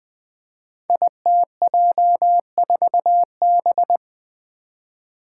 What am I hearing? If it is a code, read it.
Morse code, heard as a keyed tone, about 20 wpm, 700 Hz. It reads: ITJ4B
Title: Five Challenging Letter-Number Combinations 20wpm